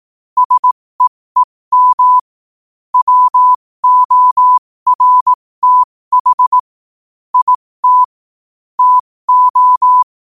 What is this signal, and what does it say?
Morse code, heard as a keyed tone, about 18 wpm, spaced as for 15 wpm, 1000 Hz. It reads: SEEM WORTH IT TO
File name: qotdfiles / 15wpm